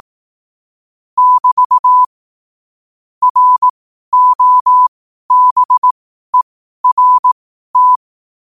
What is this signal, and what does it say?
Morse code, heard as a keyed tone, about 18 wpm, spaced as for 12 wpm, 1000 Hz. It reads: <BT> ROBERT
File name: qotdfiles / 12wpm